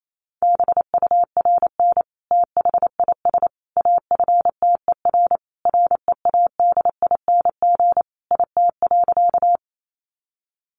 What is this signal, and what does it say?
Morse code, heard as a keyed tone, about 28 wpm, 700 Hz. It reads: BURN THIS AFTER READING IT.
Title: Sentences from Top 1300 Words 28wpm